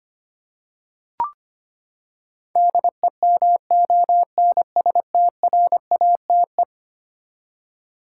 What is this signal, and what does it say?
Morse code, heard as a keyed tone, about 25 wpm, 700 Hz. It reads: DEMONSTRATE
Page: morse-code-ninja-practice-25wpm